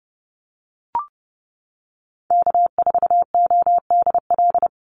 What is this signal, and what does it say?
Morse code, heard as a keyed tone, about 30 wpm, 700 Hz. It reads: K4ODL
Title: Call Signs - 30wpm